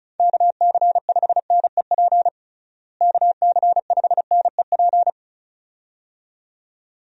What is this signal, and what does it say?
Morse code, heard as a keyed tone, about 36 wpm, 700 Hz. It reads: KC5NEP KC5NEP
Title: USA Call Signs - 6 Characters - Repeated 2x - 35wpm